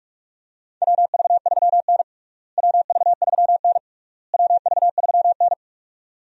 Morse code, heard as a keyed tone, about 45 wpm, 700 Hz. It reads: WV3N WV3N WV3N
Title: USA Call Signs - 4 Characters - Repeated 3x - 45wpm